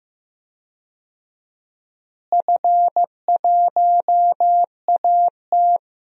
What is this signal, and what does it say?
Morse code, heard as a keyed tone, about 16 wpm, 700 Hz. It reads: F1AT
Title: DXCC-100 Call Signs - 4 Characters - Repeated 3x - 15wpm